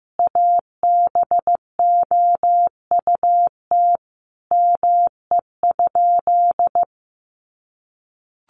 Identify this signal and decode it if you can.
Morse code, heard as a keyed tone, about 14 wpm, 700 Hz. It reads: ABOUT ME?
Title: Sentences from Top 200 Words 15wpm